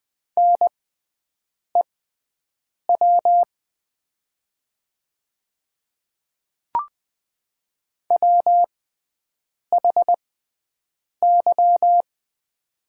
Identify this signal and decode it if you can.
Morse code, heard as a keyed tone, about 20 wpm, spaced as for 7 wpm, 700 Hz. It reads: NEW WHY
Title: 6x Character Spacing - Top 300 Words - 20wpm